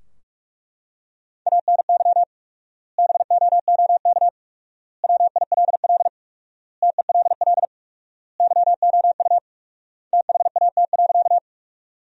Morse code, heard as a keyed tone, about 42 wpm, 700 Hz. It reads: ANY BOOK WILL TELL YOU THAT.
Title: 4x Word-Spacing - Sentences from Top 300 Words - 45wpm